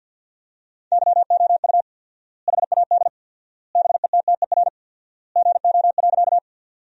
Morse code, heard as a keyed tone, about 50 wpm, 700 Hz. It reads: YOU HAD BETTER GO.
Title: 4x Word-Spacing - Sentences from Top 400 Words - 50wpm